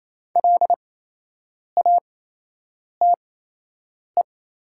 Morse code, heard as a keyed tone, about 28 wpm, spaced as for 8 wpm, 700 Hz. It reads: LATE
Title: 8x Character Spacing - Top 300 Words - 28wpm